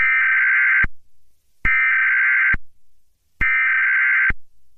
Emergency Alert System (SAME) tones (header bursts), heard as harsh buzzing bursts. It is heard from 0.0 to 4.3 s.
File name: eas-alert-usa